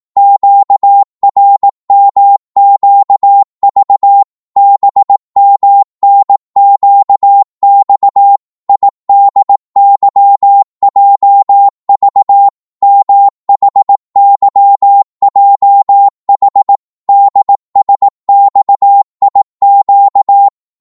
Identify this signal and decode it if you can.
Morse code, heard as a keyed tone, about 18 wpm, 800 Hz. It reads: QRMQV BMNQX IDYJV MHYJH DSXIQ
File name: Morse / ltr-18wpm